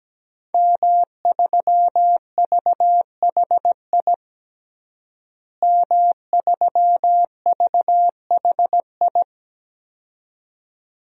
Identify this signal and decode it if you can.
Morse code, heard as a keyed tone, about 17 wpm, 700 Hz. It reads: M3VHI M3VHI
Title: DXCC-100 Call Signs - 5 Characters - Repeated 2x - 17wpm